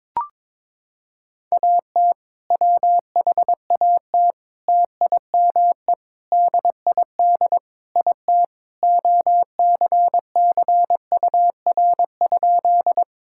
Morse code, heard as a keyed tone, about 22 wpm, 700 Hz. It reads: AT WHAT TIME DID IT OCCUR?
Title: Sentences from Top 1000 Words - Review Part 6 - 22wpm